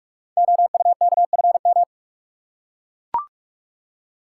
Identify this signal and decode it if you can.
Morse code, heard as a keyed tone, about 44 wpm, 690 Hz. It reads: OUKFM
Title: Code Groups - 5 Letters - 45wpm